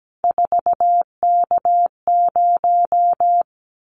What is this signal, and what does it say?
Morse code, heard as a keyed tone, about 17 wpm, 700 Hz. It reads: 4K0
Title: Callsign Prefixes for Top 100 DXCC - 17wpm